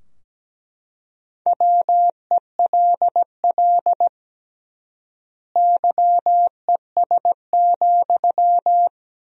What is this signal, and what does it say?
Morse code, heard as a keyed tone, about 17 wpm, 700 Hz. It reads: WELL YES,
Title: 3x Word-Spacing - Sentences from Top 500 Words - 17wpm